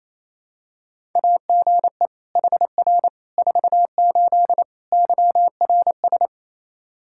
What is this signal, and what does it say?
Morse code, heard as a keyed tone, about 27 wpm, 700 Hz. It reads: AGE HR 48 YRS
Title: Practice QSO - Exchange 3 - More Info - 28wpm